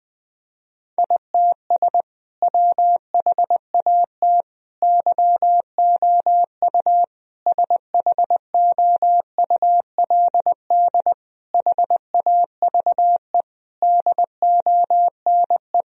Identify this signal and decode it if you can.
Morse code, heard as a keyed tone, about 21 wpm, 700 Hz. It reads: ITS WHAT YOU SHOULD HAVE DONE
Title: Sentences from Top 1000 Words - Review Part 3 - 20wpm